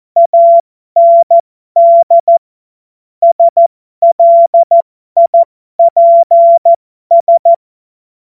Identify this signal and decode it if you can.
Morse code, heard as a keyed tone, about 14 wpm, spaced as for 12 wpm, 680 Hz. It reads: AND SLIPS